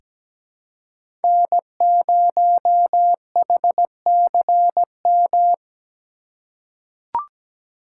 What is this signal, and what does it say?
Morse code, heard as a keyed tone, about 17 wpm, 700 Hz. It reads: N0HCM
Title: USA Call Signs - 5 Characters - 3x Character Spacing - 17wpm